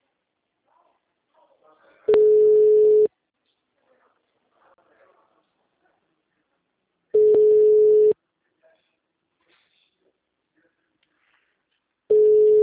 Ringback tone (Europe, 425 Hz).
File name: out- -2009- - - .36846